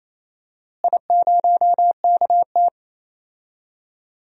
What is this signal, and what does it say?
Morse code, heard as a keyed tone, about 28 wpm, 700 Hz. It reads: I0KT
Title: DXCC-100 Call Signs - 4 Characters - Repeated 3x - 28wpm